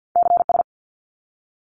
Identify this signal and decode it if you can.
Morse code, heard as a keyed tone, about 50 wpm, 700 Hz. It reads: CS
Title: Abbreviations 50wpm